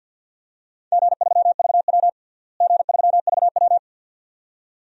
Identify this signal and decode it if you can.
Morse code, heard as a keyed tone, about 51 wpm, 700 Hz. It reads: G3VW G3VW
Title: DXCC-100 Call Signs - 4 Characters - Repeated 2x - 50wpm